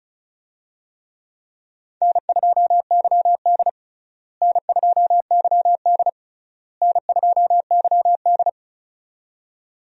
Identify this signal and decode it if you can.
Morse code, heard as a keyed tone, about 34 wpm, 700 Hz. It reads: N2YD N2YD N2YD
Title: USA Call Signs - 4 Characters - Repeated 3x - 35wpm